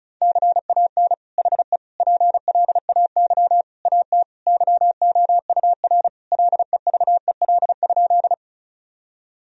Morse code, heard as a keyed tone, about 36 wpm, 700 Hz. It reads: CAN HE PLAY AT YOUR LEVEL?
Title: Sentences from Top 1000 Words - Review Part 4 - 35wpm